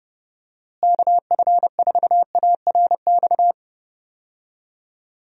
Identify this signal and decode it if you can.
Morse code, heard as a keyed tone, about 30 wpm, 700 Hz. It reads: KF4ARX
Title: USA Call Signs - 6 Characters - Repeated 2x - 30wpm